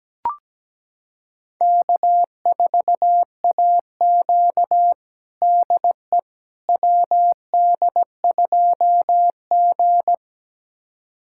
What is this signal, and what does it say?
Morse code, heard as a keyed tone, about 17 wpm, 700 Hz. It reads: K4AQ DE WD2G
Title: USA Callsign DE Callsign - 4 Characters - 17wpm